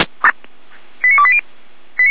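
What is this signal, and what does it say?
Five-tone selective call (ZVEI), heard as quick runs of five tones, tones of about 70 ms. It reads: 89280
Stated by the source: ZVEI five-tone selective call received on SDR.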